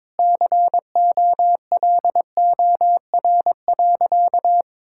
Morse code, heard as a keyed tone, about 22 wpm, 700 Hz. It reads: COLOR.
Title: Sentences from Top 1600 Words 22wpm